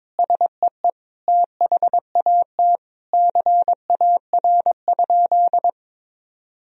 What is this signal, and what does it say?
Morse code, heard as a keyed tone, about 22 wpm, 700 Hz. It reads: SEE THAT CAR?
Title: Sentences from Top 1000 Words - Review Part 1 - 22wpm